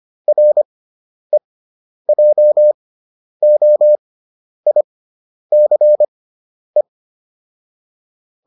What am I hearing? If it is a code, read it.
Morse code, heard as a keyed tone, about 25 wpm, spaced as for 10 wpm, 600 Hz. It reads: REJOICE